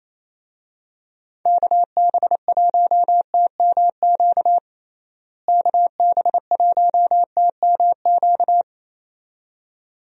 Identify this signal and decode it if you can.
Morse code, heard as a keyed tone, about 28 wpm, 700 Hz. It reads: KB1TMQ KB1TMQ
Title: USA Call Signs - 6 Characters - Repeated 2x - 28wpm